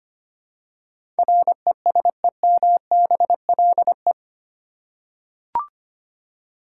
Morse code, heard as a keyed tone, about 25 wpm, 700 Hz. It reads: RESEMBLE
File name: 8 Letter Words - First 3 Letters Given - 25wpm